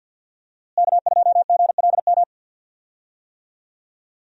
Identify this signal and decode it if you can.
Morse code, heard as a keyed tone, about 51 wpm, 700 Hz. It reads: K2GLM
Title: USA Call Signs - 5 Characters - Repeated 3x - 50wpm